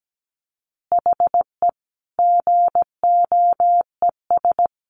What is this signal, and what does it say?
Morse code, heard as a keyed tone, about 17 wpm, 700 Hz. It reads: HE GOES
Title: Easy Phrases 17wpm